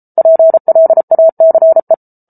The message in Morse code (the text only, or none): PLACE